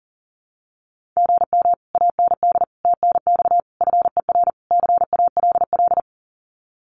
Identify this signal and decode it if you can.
Morse code, heard as a keyed tone, about 40 wpm, 700 Hz. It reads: GM AND TNX FER CALL
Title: Common QSO Phrases Encore 40wpm